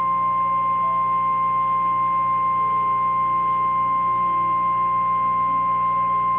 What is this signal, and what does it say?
Emergency Alert System (SAME) tones (the 1050 Hz attention tone). It is heard from 0.4 to 5.6 s.
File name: eas / KNCYKBIE / KNCYKBIE